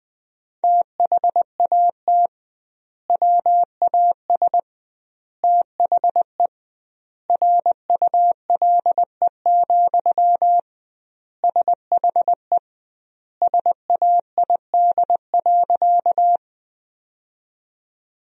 Morse code, heard as a keyed tone, about 20 wpm, 700 Hz. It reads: THAT WAS THE RULE, SHE SAID.
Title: 2x Word-Spacing - Sentences from Top 400 Words - 20wpm